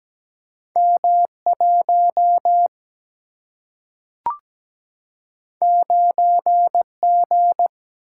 Morse code, heard as a keyed tone, about 17 wpm, 700 Hz. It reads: M 1 9 G
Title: Single-Letter Single-Number 17wpm